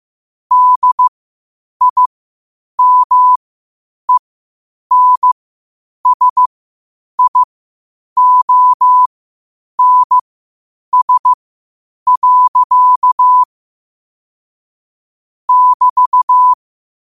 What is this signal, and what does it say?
Morse code, heard as a keyed tone, about 15 wpm, spaced as for 8 wpm, 1000 Hz. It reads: DIMENSIONS. <BT>
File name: qotdfiles / 8wpm